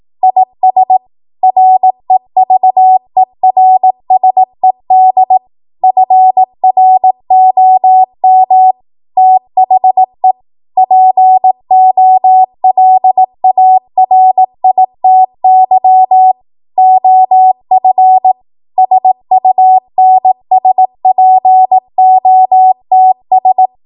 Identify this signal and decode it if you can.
Morse code, heard as a keyed tone, about 18 wpm, 750 Hz. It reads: IS REVERSED FROM THE POLARITY OF SUNSPOTS
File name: Archive / 18 WPM / 18WPM